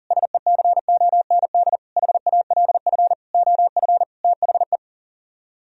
Morse code, heard as a keyed tone, about 52 wpm, spaced as for 46 wpm, 700 Hz. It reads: SECOND HALF OF THE
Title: Speed-Racing - Sets of 5 Words 40wpm